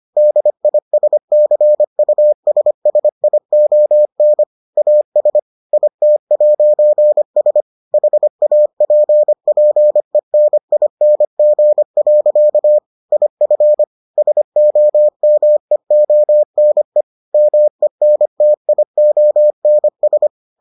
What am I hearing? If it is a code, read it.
Morse code, heard as a keyed tone, about 25 wpm, 600 Hz. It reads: DISCUSSION AS IT'S HAPPENING. IF SOMEONE MENTIONS